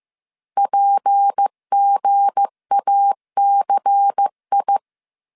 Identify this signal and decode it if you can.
Morse code, heard as a keyed tone, about 15 wpm, 800 Hz. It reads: PGACI
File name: CW / DET / 1 9 1500C S A 14wpm 3222